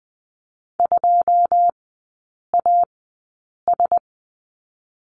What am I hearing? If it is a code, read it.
Morse code, heard as a keyed tone, about 20 wpm, spaced as for 8 wpm, 700 Hz. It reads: 2AS
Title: 081 Lesson 14e - Number 2 - Three Characters 20wpm